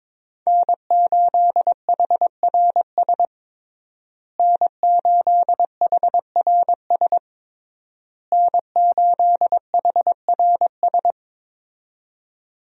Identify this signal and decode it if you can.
Morse code, heard as a keyed tone, about 22 wpm, 700 Hz. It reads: N8HRS N8HRS N8HRS
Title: USA Call Signs - 5 Characters - Repeated 3x - 22wpm